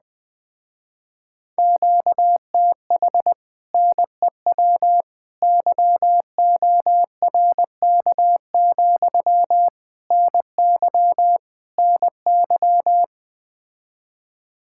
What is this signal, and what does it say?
Morse code, heard as a keyed tone, about 20 wpm, 700 Hz. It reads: QTH NEW YORK, NY NY
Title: Speed-Racing Common QSO Phrases - 17wpm